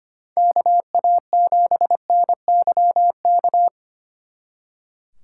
Morse code, heard as a keyed tone, about 31 wpm, 700 Hz. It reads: KA7NYK
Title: Speed Racing - USA 2x3 Call Signs 25wpm